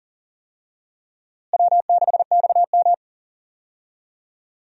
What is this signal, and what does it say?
Morse code, heard as a keyed tone, about 40 wpm, 700 Hz. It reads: W6XM
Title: USA Call Signs - 4 Characters - Repeated 2x - 40wpm